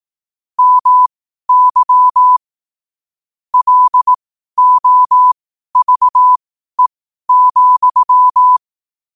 Morse code, heard as a keyed tone, about 18 wpm, spaced as for 12 wpm, 1000 Hz. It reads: MY LOVE,
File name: qotdfiles / 12wpm